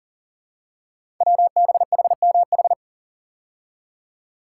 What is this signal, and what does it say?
Morse code, heard as a keyed tone, about 42 wpm, 700 Hz. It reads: WBHMH
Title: Code Groups - 5 Letter:Numbers - 40wpm